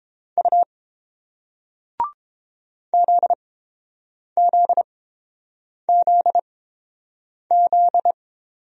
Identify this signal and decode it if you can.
Morse code, heard as a keyed tone, about 30 wpm, spaced as for 7 wpm, 700 Hz. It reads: U ZZZZ